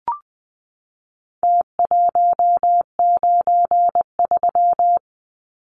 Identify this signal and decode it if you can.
Morse code, heard as a keyed tone, about 20 wpm, 700 Hz. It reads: T193
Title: Numbers - Contest Style - 4-Digit 20wpm